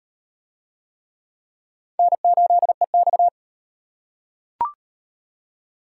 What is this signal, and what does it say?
Morse code, heard as a keyed tone, about 38 wpm, 700 Hz. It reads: N8EX